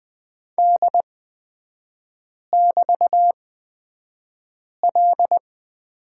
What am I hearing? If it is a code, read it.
Morse code, heard as a keyed tone, about 20 wpm, spaced as for 5 wpm, 700 Hz. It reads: D<BT>L
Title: 224 Lesson 36e - BT - Three Characters 20wpm